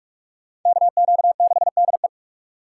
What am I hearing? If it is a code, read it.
Morse code, heard as a keyed tone, about 44 wpm, 700 Hz. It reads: KQ6DE